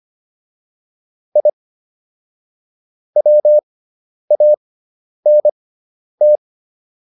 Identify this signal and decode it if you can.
Morse code, heard as a keyed tone, about 25 wpm, spaced as for 10 wpm, 600 Hz. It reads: I WANT